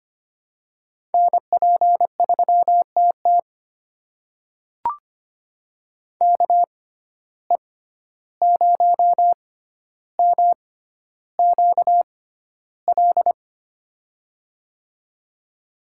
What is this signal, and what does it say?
Morse code, heard as a keyed tone, about 25 wpm, spaced as for 12 wpm, 700 Hz. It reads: NP3TT KE0MQL
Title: USA Call Signs - 6x Character Spacing - 25wpm